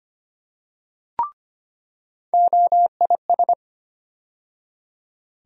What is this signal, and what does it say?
Morse code, heard as a keyed tone, about 25 wpm, 700 Hz. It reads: OIS
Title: USA Call Sign Postfixes - 3 Letters 25wpm